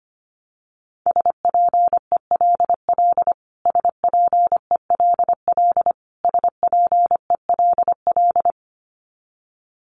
Morse code, heard as a keyed tone, about 25 wpm, 700 Hz. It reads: SPELL SPELL SPELL
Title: Top 200 Words Repeated 25wpm